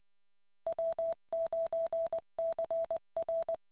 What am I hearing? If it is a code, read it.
Morse code, heard as a keyed tone, about 20 wpm, 660 Hz. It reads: W9CR